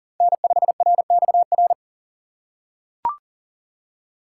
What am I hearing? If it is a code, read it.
Morse code, heard as a keyed tone, about 40 wpm, 700 Hz. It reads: N5RXR
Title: USA Call Signs - 4x Character Spacing - 40wpm